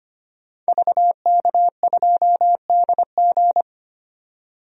Morse code, heard as a keyed tone, about 25 wpm, 700 Hz. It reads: VK2DG
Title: DXCC-100 Call Signs - 5 Characters - 25wpm